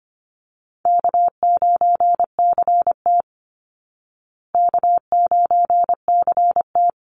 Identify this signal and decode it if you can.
Morse code, heard as a keyed tone, about 25 wpm, 700 Hz. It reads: K9CT K9CT
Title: Easy Call Signs Repeated 2x - 25wpm